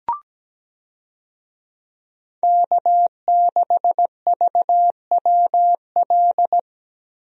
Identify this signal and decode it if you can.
Morse code, heard as a keyed tone, about 17 wpm, 700 Hz. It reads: K6VWL
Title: USA Call Signs - 5 Characters - Repeated 2x - 17wpm